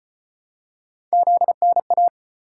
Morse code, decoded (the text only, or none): ZNA